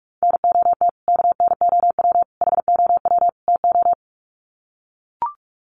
Morse code, heard as a keyed tone, about 44 wpm, 700 Hz. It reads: NOT KNOW HOW TO
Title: Sets of 4 Words 45wpm